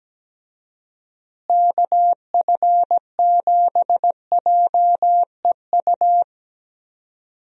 Morse code, heard as a keyed tone, about 17 wpm, 700 Hz. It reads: KF7JEU
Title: USA Call Signs - 6 Characters - Repeated 3x - 17wpm